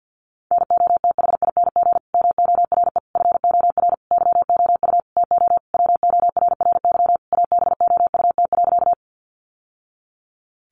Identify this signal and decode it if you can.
Morse code, heard as a keyed tone, about 50 wpm, 700 Hz. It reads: NOTHING MORE FOR YOU TO WORRY ABOUT.
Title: Sentences from Top 1100 Words 50wpm